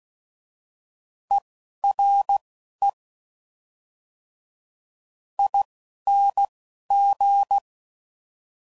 Morse code, heard as a keyed tone, about 16 wpm, spaced as for 11 wpm, 800 Hz. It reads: ERE ING
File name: Suffixes / General / ERE INGs